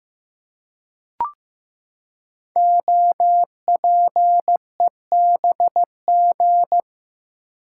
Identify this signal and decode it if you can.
Morse code, heard as a keyed tone, about 15 wpm, 700 Hz. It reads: OPEBG
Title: Code Groups - 5 Letter:Numbers - 15wpm